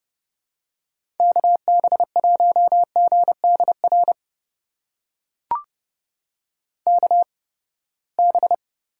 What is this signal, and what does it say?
Morse code, heard as a keyed tone, about 30 wpm, spaced as for 21 wpm, 700 Hz. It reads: KB1GDR KB
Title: USA Call Signs - 6 Characters - 8x Character Spacing - 30wpm